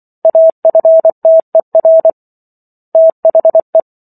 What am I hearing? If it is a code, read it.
Morse code, heard as a keyed tone, about 24 wpm, 650 Hz. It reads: AFTER THE